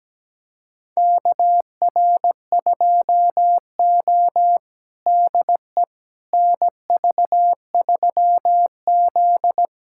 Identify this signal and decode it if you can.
Morse code, heard as a keyed tone, about 17 wpm, 700 Hz. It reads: KR2O DE NV3Z
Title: USA Callsign DE Callsign - 4 Characters - 17wpm